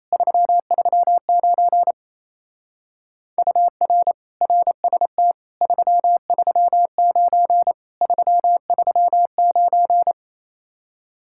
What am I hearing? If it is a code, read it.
Morse code, heard as a keyed tone, about 27 wpm, 700 Hz. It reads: 339 UR RST 339 339
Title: Speed-Racing Common QSO Phrases Encore - 28wpm